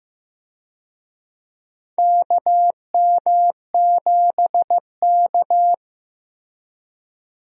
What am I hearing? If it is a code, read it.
Morse code, heard as a keyed tone, about 15 wpm, 700 Hz. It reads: KM7K
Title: USA Call Signs - 4 Characters - Repeated 3x - 15wpm